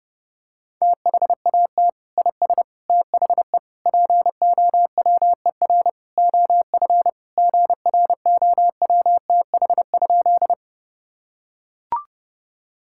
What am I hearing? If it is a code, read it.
Morse code, heard as a keyed tone, about 30 wpm, 700 Hz. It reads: THAT IS THE POWER OF GROWTH?
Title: Sentences from Top 1000 Words - Review Part 5 - 30wpm